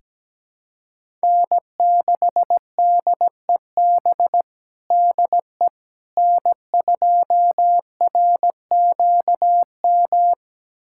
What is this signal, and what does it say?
Morse code, heard as a keyed tone, about 17 wpm, 700 Hz. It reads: N6DEB DE N2RQM
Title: USA Callsign DE Callsign - 5 Characters - 17wpm